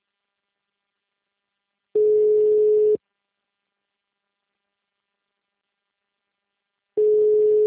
Ringback tone (Europe, 425 Hz).
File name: out- -2001- - - .29485